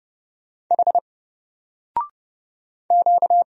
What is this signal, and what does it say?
Morse code, heard as a keyed tone, about 30 wpm, 700 Hz. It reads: H Q
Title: Single Character (Letter:Number:Punctuation:Prosign) 30wpm